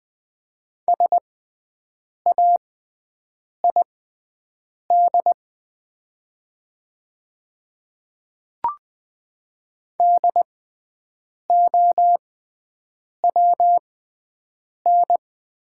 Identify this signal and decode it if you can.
Morse code, heard as a keyed tone, about 20 wpm, spaced as for 7 wpm, 700 Hz. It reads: SAID DOWN
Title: 6x Character Spacing - Top 100 Words - 20wpm